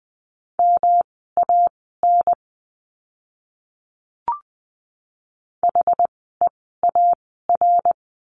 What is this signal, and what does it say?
Morse code, heard as a keyed tone, about 20 wpm, spaced as for 14 wpm, 700 Hz. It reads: MAN HEAR
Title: 206 Lesson 33f - Letter j - Words 20wpm